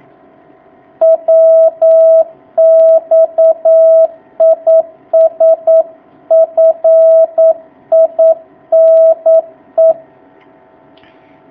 Morse code, heard as a keyed tone, about 9 wpm, 650 Hz. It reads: WX IS FINE
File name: jh8cbh / 2019cw